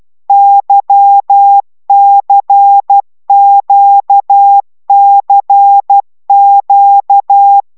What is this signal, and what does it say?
Morse code, heard as a keyed tone, about 12 wpm, 800 Hz. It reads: YCQCQ
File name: taviro tanulas / cw 75 15 10